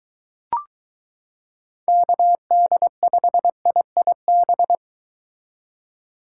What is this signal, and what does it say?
Morse code, heard as a keyed tone, about 22 wpm, 700 Hz. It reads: KD5IIB